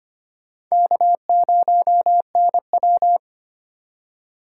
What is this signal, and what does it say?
Morse code, heard as a keyed tone, about 25 wpm, 700 Hz. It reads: K0NW
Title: K1USN SST QSO Practice - 25wpm